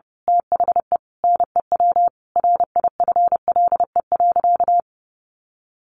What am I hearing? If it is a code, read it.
Morse code, heard as a keyed tone, about 30 wpm, 700 Hz. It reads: THE NEW RIFLE.
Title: A Sound Of Thunder 30wpm